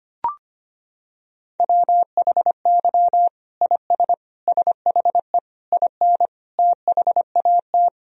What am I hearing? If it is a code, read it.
Morse code, heard as a keyed tone, about 25 wpm, 700 Hz. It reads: WHY IS SHE IN THAT
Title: Sentences from Top 1000 Words - Review Part 2 - 25wpm